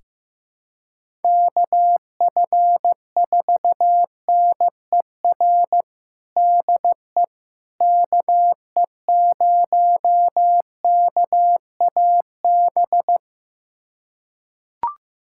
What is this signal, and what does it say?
Morse code, heard as a keyed tone, about 16 wpm, 700 Hz. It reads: KF4NER DE KE0KAB
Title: USA Callsign DE Callsign 15wpm